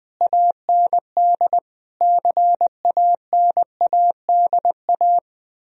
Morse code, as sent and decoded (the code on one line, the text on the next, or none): .- -. -.. / -.-. .- -. .- -.. .-
AND CANADA